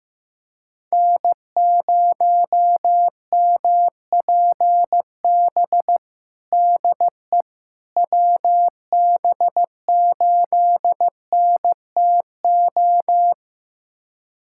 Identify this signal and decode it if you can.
Morse code, heard as a keyed tone, about 15 wpm, 700 Hz. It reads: N0MPB DE WB8NTO
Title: USA Callsign DE Callsign 15wpm